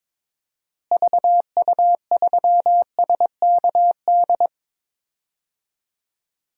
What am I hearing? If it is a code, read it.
Morse code, heard as a keyed tone, about 22 wpm, 700 Hz. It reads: VU3SKD
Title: DXCC-100 Call Signs - 6 Characters - Repeated 2x - 22wpm